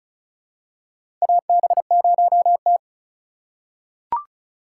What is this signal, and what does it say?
Morse code, heard as a keyed tone, about 36 wpm, 700 Hz. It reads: AB0T
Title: USA Call Signs - 4x Character Spacing - 35wpm